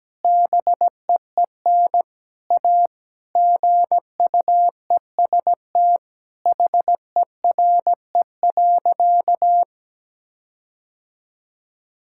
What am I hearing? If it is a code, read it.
Morse code, heard as a keyed tone, about 17 wpm, 700 Hz. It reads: BEEN A GUEST HERE.
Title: Sentences from Top 1600 Words 17wpm